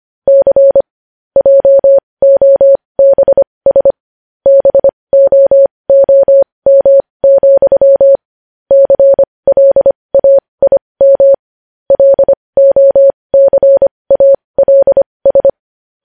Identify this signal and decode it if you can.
Morse code, heard as a keyed tone, about 25 wpm, spaced as for 20 wpm, 550 Hz. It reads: C JOBS BOOM, CLAIM LOCALS
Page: news headlines in morse code at 20 wpm